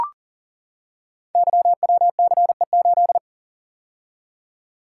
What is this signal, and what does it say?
Morse code, heard as a keyed tone, about 41 wpm, 700 Hz. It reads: YWCE8
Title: Code Groups - 5 Letter:Numbers - 40wpm